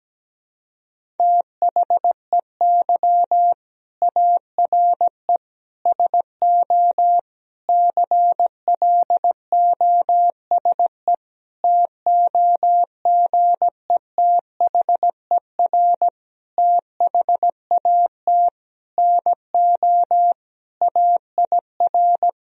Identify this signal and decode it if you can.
Morse code, heard as a keyed tone, about 17 wpm, 700 Hz. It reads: THEY ARE SO CLOSE TOGETHER THAT NO AIR
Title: Sentences from Top 1000 Words - Review Part 1 - 17wpm